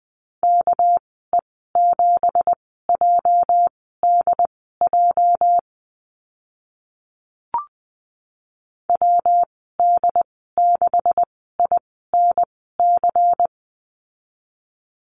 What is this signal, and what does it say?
Morse code, heard as a keyed tone, about 20 wpm, spaced as for 14 wpm, 700 Hz. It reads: KE7JDJ WD6INC
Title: 238 Lesson 38g - Letter q - Call Signs 20wpm